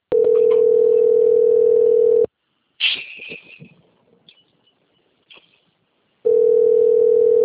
Ringback tone (North America), 440 + 480 Hz.